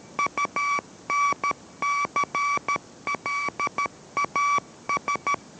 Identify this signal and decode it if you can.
Morse code, heard as a keyed tone, about 13 wpm, 1140 Hz. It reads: UNCLAS